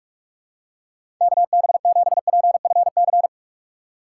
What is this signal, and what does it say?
Morse code, heard as a keyed tone, about 46 wpm, 700 Hz. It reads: KB7PFC
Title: USA Call Signs - 4x Character Spacing - 45wpm